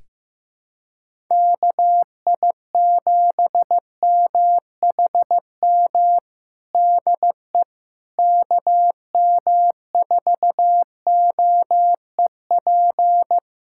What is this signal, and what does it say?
Morse code, heard as a keyed tone, about 15 wpm, 700 Hz. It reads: KI7MHM DE KM4OEP
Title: USA Callsign DE Callsign - 6 Characters - 15wpm